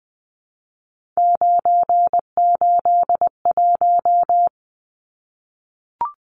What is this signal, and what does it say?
Morse code, heard as a keyed tone, about 20 wpm, 700 Hz. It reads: 981
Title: Numbers - 3-Digit 20wpm